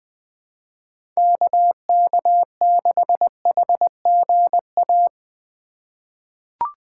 Morse code, heard as a keyed tone, about 21 wpm, 700 Hz. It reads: KK6HGA
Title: USA Call Signs - 6 Characters - 6x Character Spacing - 20wpm